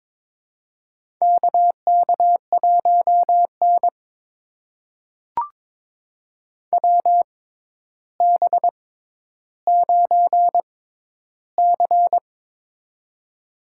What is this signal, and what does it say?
Morse code, heard as a keyed tone, about 22 wpm, spaced as for 11 wpm, 700 Hz. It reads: KK1N WB9C
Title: USA Call Signs - 6x Character Spacing - 22wpm